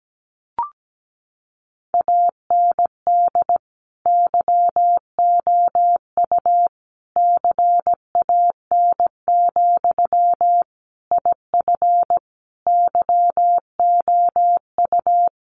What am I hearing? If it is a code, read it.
Morse code, heard as a keyed tone, about 17 wpm, 700 Hz. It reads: AND YOU CAN, IF YOU
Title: Sentences from Top 1500 Words 17wpm